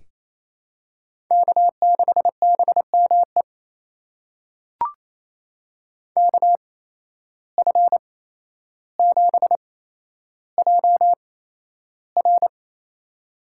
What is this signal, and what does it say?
Morse code, heard as a keyed tone, about 28 wpm, spaced as for 12 wpm, 700 Hz. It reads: K6BME KF7JR
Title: USA Call Signs - 8x Character Spacing - 28wpm